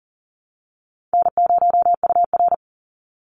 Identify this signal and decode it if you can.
Morse code, heard as a keyed tone, about 40 wpm, 700 Hz. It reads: N0UR
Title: Easy Call Signs Repeated 2x - 40wpm